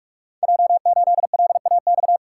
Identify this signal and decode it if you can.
Morse code, heard as a keyed tone, about 44 wpm, 700 Hz. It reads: J8LAX
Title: Code Groups - 5 Letter:Numbers - 45wpm